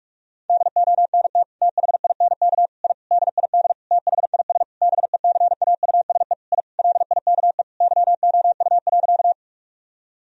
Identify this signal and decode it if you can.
Morse code, heard as a keyed tone, about 41 wpm, 700 Hz. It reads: DONT THINK I DID THIS BECAUSE I LIKE YOU.
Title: Sentences from Top 1000 Words - Review Part 3 - 45wpm